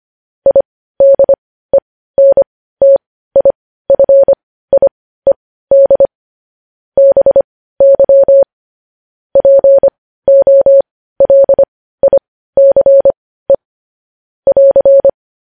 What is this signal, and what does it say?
Morse code, heard as a keyed tone, about 25 wpm, spaced as for 15 wpm, 550 Hz. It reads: IDENTIFIED BY POLICE <AR>